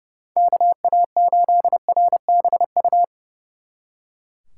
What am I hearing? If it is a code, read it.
Morse code, heard as a keyed tone, about 37 wpm, 700 Hz. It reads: KA8RBU
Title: Speed Racing - USA 2x3 Call Signs 30wpm